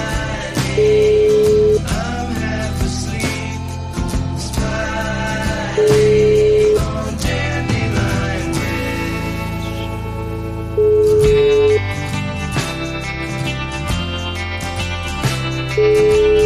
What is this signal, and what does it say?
Ringback tone (Europe, 425 Hz), with noise.